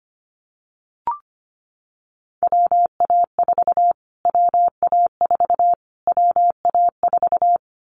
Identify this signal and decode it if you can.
Morse code, heard as a keyed tone, about 25 wpm, 700 Hz. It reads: WA4 WA4 WA4
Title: Callsign Prefixes for Top 25 DXCC - Repeated 3x - 25wpm